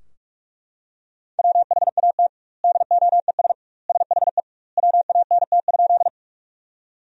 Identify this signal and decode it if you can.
Morse code, heard as a keyed tone, about 42 wpm, 700 Hz. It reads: WHAT DOES SHE WANT?
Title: 2x Word-Spacing - Sentences from Top 200 Words - 45wpm